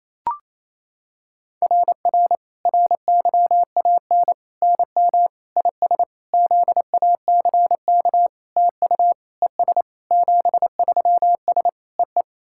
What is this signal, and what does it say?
Morse code, heard as a keyed tone, about 28 wpm, 700 Hz. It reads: RR RYAN NM IS ZACK TU ES 73S EE
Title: SOTA QSO Practice 28wpm